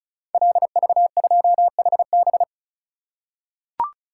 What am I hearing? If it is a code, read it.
Morse code, heard as a keyed tone, about 34 wpm, 700 Hz. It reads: LV2HB
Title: Code Groups - 5 Letter:Numbers - 35wpm